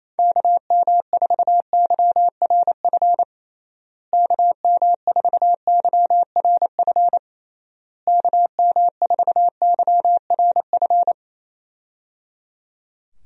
Morse code, heard as a keyed tone, about 27 wpm, 700 Hz. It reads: KM4YRF KM4YRF KM4YRF
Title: USA Call Signs - 6 Characters - Repeated 3x - 28wpm